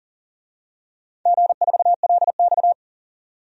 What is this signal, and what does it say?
Morse code, heard as a keyed tone, about 40 wpm, 700 Hz. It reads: G4LX
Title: DXCC-100 Call Signs - 4 Characters - 40wpm